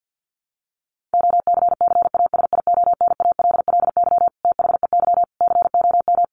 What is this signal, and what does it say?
Morse code, heard as a keyed tone, about 48 wpm, 700 Hz. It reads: OCCASIONALLY THEY COM
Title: Sentences from Top 1300 Words 50wpm